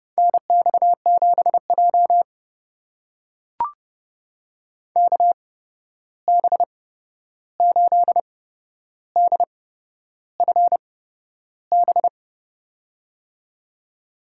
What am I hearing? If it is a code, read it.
Morse code, heard as a keyed tone, about 32 wpm, spaced as for 11 wpm, 700 Hz. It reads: NX7J KB8DFB
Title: USA Call Signs - 8x Character Spacing - 30wpm